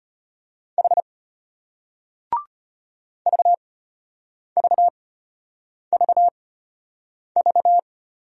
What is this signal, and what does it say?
Morse code, heard as a keyed tone, about 34 wpm, spaced as for 8 wpm, 700 Hz. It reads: H VVVV